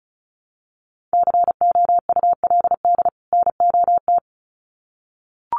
Morse code, heard as a keyed tone, about 34 wpm, 700 Hz. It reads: COULD NOT
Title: Sets of 2 Words 35wpm